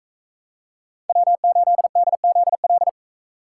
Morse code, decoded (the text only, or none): W8DZL